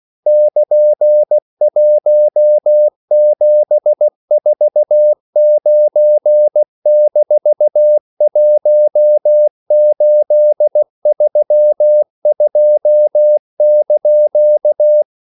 Morse code, heard as a keyed tone, about 16 wpm, 600 Hz. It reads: <KN>1749-1832)